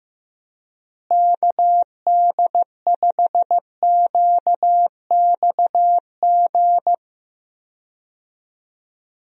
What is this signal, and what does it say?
Morse code, heard as a keyed tone, about 15 wpm, 700 Hz. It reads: KD5QXG
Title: USA Call Signs - 6 Characters - Repeated 3x - 15wpm